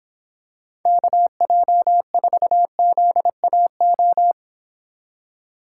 Morse code, heard as a keyed tone, about 26 wpm, 700 Hz. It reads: KJ4ZAO